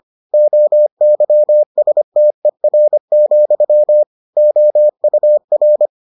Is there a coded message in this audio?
Yes: Morse code from 0.3 to 5.9 s, heard as a keyed tone, about 25 wpm, 600 Hz. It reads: OYSTER, OUR